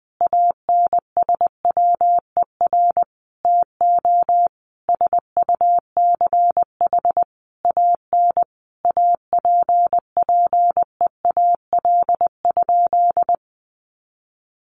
Morse code, heard as a keyed tone, about 20 wpm, 700 Hz. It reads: ANSWER TO SUCH AN APPEAL?
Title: Sentences from Top 1300 Words 20wpm